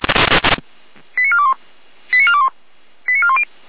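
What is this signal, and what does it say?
Five-tone selective call (ZVEI), heard as quick runs of five tones, tones of about 70 ms. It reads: 89421 89421 89410
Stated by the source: ZVEI five-tone selective call, code 88074, received via SDR.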